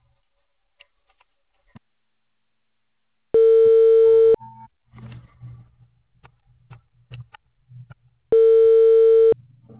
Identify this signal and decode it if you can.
Ringback tone (China, 450 Hz).